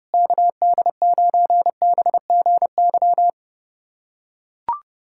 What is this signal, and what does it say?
Morse code, heard as a keyed tone, about 30 wpm, 700 Hz. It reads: KD9BGY
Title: USA Callsign DE Callsign - 6 Characters - 30wpm